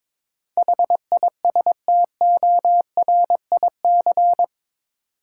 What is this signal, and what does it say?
Morse code, heard as a keyed tone, about 22 wpm, 700 Hz. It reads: HISTORIC
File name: practice / 22wpm / Eight Letter Words - 22wpm